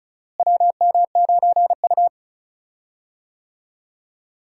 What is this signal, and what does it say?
Morse code, heard as a keyed tone, about 36 wpm, 700 Hz. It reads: WM9U
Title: USA Call Signs - 4 Characters - Repeated 2x - 35wpm